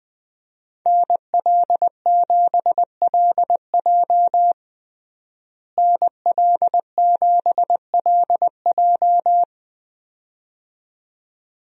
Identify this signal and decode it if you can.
Morse code, heard as a keyed tone, about 21 wpm, 700 Hz. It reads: NL7LJ NL7LJ
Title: USA Call Signs - 5 Characters - Repeated 2x - 20wpm